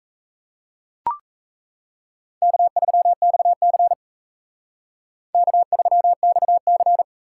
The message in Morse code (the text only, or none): K3XC K3XC